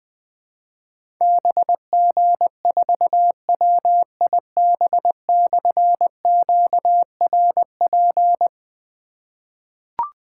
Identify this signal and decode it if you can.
Morse code, heard as a keyed tone, about 20 wpm, 700 Hz. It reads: BG4WIB/QRP
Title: Difficult Call Signs 20wpm